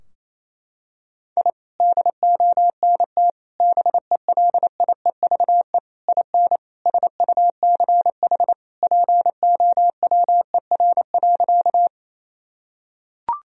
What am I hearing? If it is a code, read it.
Morse code, heard as a keyed tone, about 28 wpm, 700 Hz. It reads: I DONT BELIEVE IN SUCH POWER.
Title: Sentences from Top 1000 Words - Review Part 2 - 28wpm